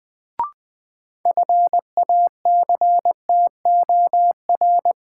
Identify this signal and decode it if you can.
Morse code, heard as a keyed tone, about 20 wpm, 700 Hz. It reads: FACTOR
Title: Sentences from Top 1300 Words - New Word Drill Alt - 20wpm